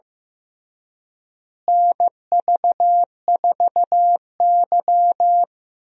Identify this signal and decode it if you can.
Morse code, heard as a keyed tone, about 15 wpm, 700 Hz. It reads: NV4Y
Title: USA Call Signs - 4 Characters - Repeated 3x - 15wpm